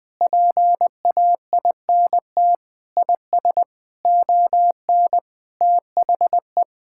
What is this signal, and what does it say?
Morse code, heard as a keyed tone, about 20 wpm, 700 Hz. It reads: PAINT IS ON THE
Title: Sentences from Top 1000 Words - Review Part 6 - 20wpm